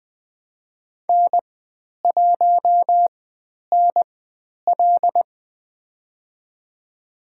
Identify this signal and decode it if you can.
Morse code, heard as a keyed tone, about 20 wpm, spaced as for 10 wpm, 700 Hz. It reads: N1NL
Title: 102 Lesson 17g - Letter u - Call Signs 20wpm